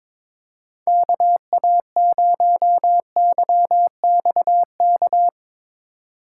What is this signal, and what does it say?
Morse code, heard as a keyed tone, about 22 wpm, 700 Hz. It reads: KA0YXK
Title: USA Call Signs - 6 Characters - 2x Character Spacing - 22wpm